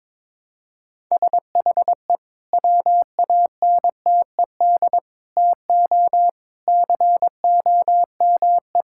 Morse code, heard as a keyed tone, about 22 wpm, 700 Hz. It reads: SHE WANTED TO COME
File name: practice / 22wpm / Sentences with 2-Word Phrases - 22wpm